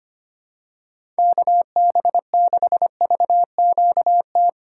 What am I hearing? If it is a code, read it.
Morse code, heard as a keyed tone, about 25 wpm, 700 Hz. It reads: KB6VQT
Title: USA Call Signs - 8x Character Spacing - 25wpm